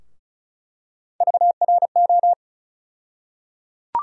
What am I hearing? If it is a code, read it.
Morse code, heard as a keyed tone, about 36 wpm, 700 Hz. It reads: VRO
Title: USA Call Sign Postfixes - 3 Letters 35wpm